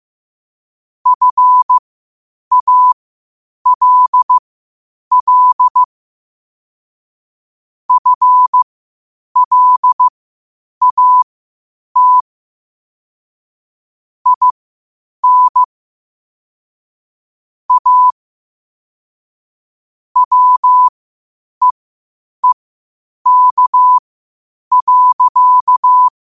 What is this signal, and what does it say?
Morse code, heard as a keyed tone, about 15 wpm, spaced as for 8 wpm, 1000 Hz. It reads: FALL FLAT IN A WEEK.
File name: qotdfiles / 8wpm